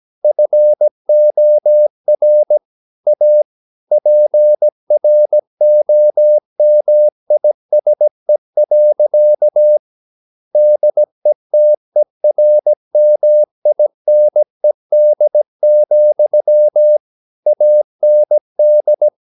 Morse code, heard as a keyed tone, about 17 wpm, 600 Hz. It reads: FOR A PROMISE. DETERMINED, AND
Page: Romanen Never af Ken Follet på engelsk.